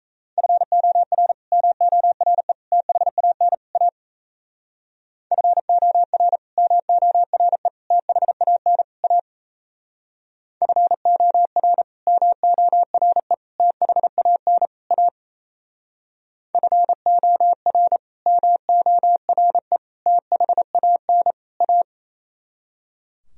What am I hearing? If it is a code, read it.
Morse code, heard as a keyed tone, about 37 wpm, 700 Hz. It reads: FOR MORE THAN A FOR MORE THAN A FOR MORE THAN A FOR MORE THAN A